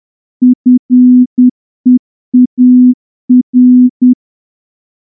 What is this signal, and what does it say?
Morse code, heard as a keyed tone, about 10 wpm, 250 Hz. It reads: FEAR
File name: morse-training-10WPM